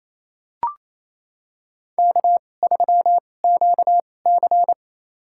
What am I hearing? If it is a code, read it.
Morse code, heard as a keyed tone, about 28 wpm, spaced as for 20 wpm, 700 Hz. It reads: K3QC
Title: USA Call Signs - 4 Characters - 2x Character Spacing - 28wpm